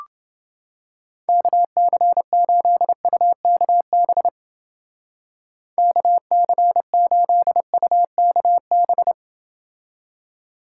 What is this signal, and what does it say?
Morse code, heard as a keyed tone, about 29 wpm, 700 Hz. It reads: KC8UKB KC8UKB